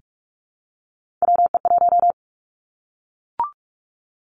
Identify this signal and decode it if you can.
Morse code, heard as a keyed tone, about 46 wpm, 700 Hz. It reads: WE1
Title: Three Challenging Letter-Number Combinations 45wpm